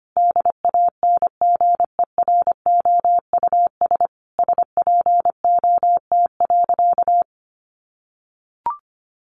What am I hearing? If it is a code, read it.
Morse code, heard as a keyed tone, about 25 wpm, 700 Hz. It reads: DANGEROUS SPOT.
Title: Sentences from Top 1200 Words 25wpm